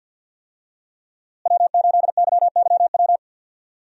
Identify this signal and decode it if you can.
Morse code, heard as a keyed tone, about 46 wpm, 700 Hz. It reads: W8YYW
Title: USA Call Signs - 5 Characters - Repeated 3x - 50wpm